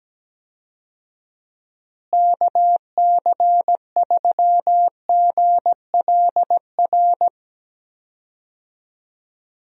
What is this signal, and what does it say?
Morse code, heard as a keyed tone, about 17 wpm, 700 Hz. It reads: KC3GLR
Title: USA Call Signs - 6 Characters - Repeated 3x - 17wpm